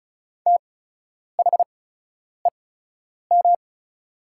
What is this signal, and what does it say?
Morse code, heard as a keyed tone, about 36 wpm, spaced as for 10 wpm, 700 Hz. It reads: THEM
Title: 8x Character Spacing - Top 300 Words - 35wpm